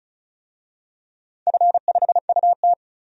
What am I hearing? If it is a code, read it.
Morse code, heard as a keyed tone, about 36 wpm, 690 Hz. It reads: F5UT
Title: DXCC-100 Call Signs - 4 Characters - 35wpm